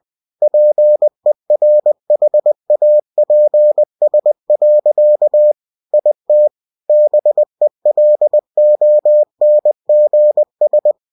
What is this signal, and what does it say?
Morse code, heard as a keyed tone, about 20 wpm, 600 Hz. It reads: PERHAPS. IT BELONGS
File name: cw-books / winnie-the-pooh / winnie-20wpm-page-0088